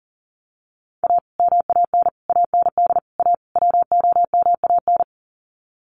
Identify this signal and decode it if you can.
Morse code, heard as a keyed tone, about 40 wpm, 700 Hz. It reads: A MAN AND A WOMAN
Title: Sets of 5 Words Encore 40wpm